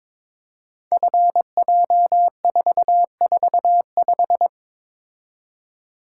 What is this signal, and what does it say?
Morse code, heard as a keyed tone, about 22 wpm, 700 Hz. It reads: FJ445
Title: Code Groups - 5 Letter:Numbers - 22wpm